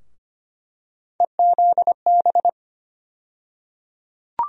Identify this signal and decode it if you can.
Morse code, heard as a keyed tone, about 25 wpm, 700 Hz. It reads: EZB
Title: USA Call Sign Postfixes - 3 Letters 25wpm